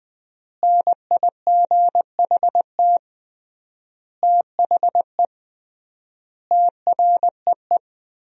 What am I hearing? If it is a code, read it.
Morse code, heard as a keyed tone, about 21 wpm, 700 Hz. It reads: NIGHT THE TREE
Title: 3x Word-Spacing - Sentences from Top 500 Words - 20wpm